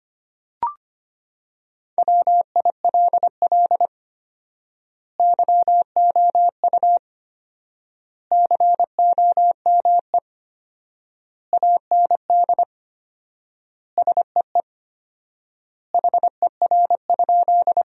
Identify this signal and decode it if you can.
Morse code, heard as a keyed tone, about 25 wpm, 700 Hz. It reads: WILL YOU COME AND SEE HER?
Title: 4x Word-Spacing - Sentences from Top 100 Words - 25wpm